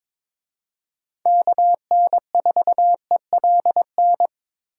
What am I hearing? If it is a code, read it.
Morse code, heard as a keyed tone, about 26 wpm, 700 Hz. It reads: KN4ELN
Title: Speed Racing - USA 2x3 Call Signs 22wpm